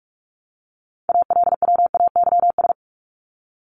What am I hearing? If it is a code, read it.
Morse code, heard as a keyed tone, about 44 wpm, 700 Hz. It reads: ALWAYS
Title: Top 300 Words Repeated 45wpm